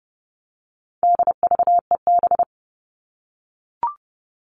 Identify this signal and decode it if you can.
Morse code, heard as a keyed tone, about 30 wpm, 700 Hz. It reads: DVEB
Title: All Cut Numbers - 4-Digit 30wpm